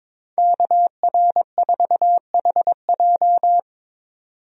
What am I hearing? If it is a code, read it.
Morse code, heard as a keyed tone, about 22 wpm, 700 Hz. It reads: KR4HJ
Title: USA Call Signs - 5 Characters - 22wpm